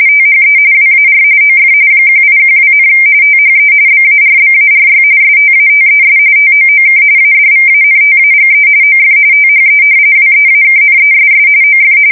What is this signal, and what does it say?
RTTY, about 45.45 baud. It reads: THE WAR OF 1812, THE BARBARY STATES TOOK ADVANTAGE OF THE WEAKNESS OF
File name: fsk170 / usnavy2